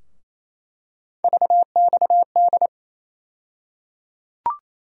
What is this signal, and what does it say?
Morse code, heard as a keyed tone, about 28 wpm, 700 Hz. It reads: VXD
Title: USA Call Sign Postfixes - 3 Letters 28wpm